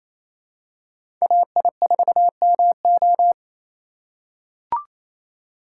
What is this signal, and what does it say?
Morse code, heard as a keyed tone, about 28 wpm, 700 Hz. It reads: AI4MO
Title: USA Call Signs - 6x Character Spacing - 28wpm